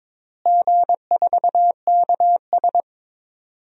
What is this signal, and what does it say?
Morse code, heard as a keyed tone, about 22 wpm, 700 Hz. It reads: G4KS
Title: DXCC-100 Call Signs - 4 Characters - Repeated 2x - 22wpm